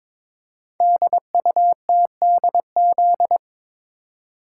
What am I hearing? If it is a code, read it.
Morse code, heard as a keyed tone, about 22 wpm, 700 Hz. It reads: DUTDZ
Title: Code Groups - 5 Letters - 22wpm